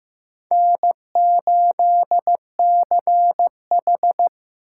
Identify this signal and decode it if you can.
Morse code, heard as a keyed tone, about 15 wpm, 700 Hz. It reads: N8CH
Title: USA Call Signs - 4 Characters - 6x Character Spacing - 15wpm